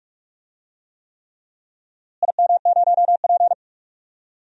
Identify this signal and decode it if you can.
Morse code, heard as a keyed tone, about 46 wpm, 700 Hz. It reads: IM0P
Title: DXCC-100 Call Signs - 4 Characters - Repeated 3x - 45wpm